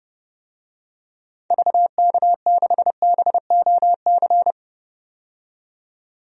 Morse code, heard as a keyed tone, about 30 wpm, 700 Hz. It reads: VK6BOC
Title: DXCC-100 Call Signs - 6 Characters - 30wpm